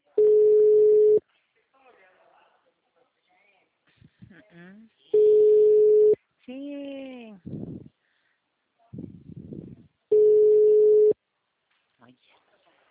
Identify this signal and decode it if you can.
Ringback tone (Europe, 425 Hz), with speech.